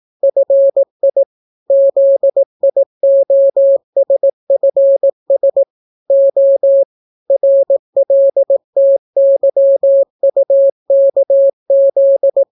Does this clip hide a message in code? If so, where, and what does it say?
Yes: Morse code from 0.2 to 12.4 s, heard as a keyed tone, about 18 wpm, 550 Hz. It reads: FI ZIOSFS O RLTYUKZ